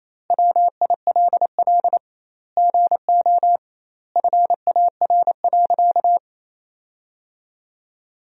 Morse code, heard as a keyed tone, about 28 wpm, 700 Hz. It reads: WILL GO FAR.
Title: 2x Word-Spacing - Sentences from Top 400 Words - 28wpm